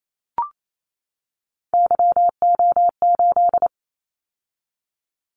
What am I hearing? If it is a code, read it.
Morse code, heard as a keyed tone, about 28 wpm, 700 Hz. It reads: YO8
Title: Callsign Prefixes for Top 25 DXCC - 28wpm